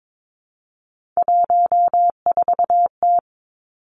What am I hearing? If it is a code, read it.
Morse code, heard as a keyed tone, about 22 wpm, 700 Hz. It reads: 14T
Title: Three Challenging Letter-Number Combinations 22wpm